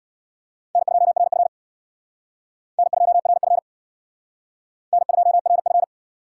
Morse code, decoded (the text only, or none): N2RF N2RF N2RF